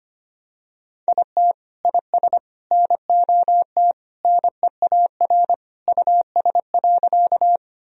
Morse code, heard as a keyed tone, about 25 wpm, 700 Hz. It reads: IT IS NOT NEAR US.
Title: Sentences from Top 1000 Words - Review Part 1 - 25wpm